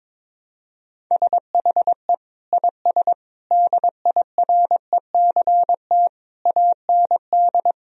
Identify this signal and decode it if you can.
Morse code, heard as a keyed tone, about 22 wpm, 700 Hz. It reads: SHE IS DIRECT AND
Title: Sentences from Top 1000 Words - Review Part 4 - 22wpm